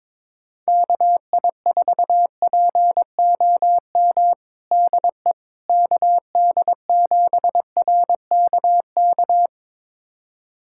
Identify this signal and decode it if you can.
Morse code, heard as a keyed tone, about 22 wpm, 700 Hz. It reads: KI4POM DE KD7RKK
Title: USA Callsign DE Callsign - 6 Characters - 22wpm